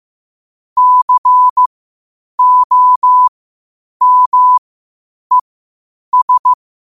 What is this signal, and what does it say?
Morse code, heard as a keyed tone, about 15 wpm, spaced as for 8 wpm, 1000 Hz. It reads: COMES
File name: qotdfiles / 8wpm